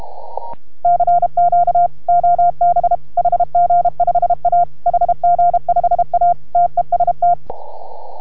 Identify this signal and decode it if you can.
Morse code, heard as a keyed tone, about 33 wpm, 690 Hz. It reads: CQ OB HG5A HG5A TEST